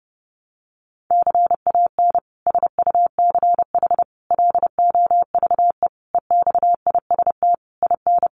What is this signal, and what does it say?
Morse code, heard as a keyed tone, about 30 wpm, 700 Hz. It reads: CAN SUCH LOVE EXIST IN
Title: Sentences from Top 1100 Words 30wpm